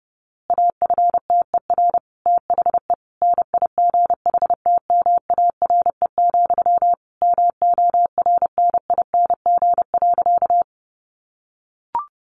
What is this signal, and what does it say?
Morse code, heard as a keyed tone, about 30 wpm, 700 Hz. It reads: AFTER THE NIGHTMARE, MORNING.
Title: A Sound Of Thunder 30wpm